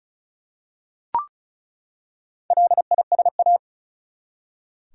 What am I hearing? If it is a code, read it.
Morse code, heard as a keyed tone, about 34 wpm, 700 Hz. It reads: LISA
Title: USA Common Names - Max 4 Letters - 35wpm